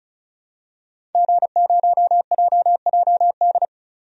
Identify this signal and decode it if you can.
Morse code, heard as a keyed tone, about 34 wpm, 700 Hz. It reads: G0JJD
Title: DXCC-100 Call Signs - 5 Characters - 35wpm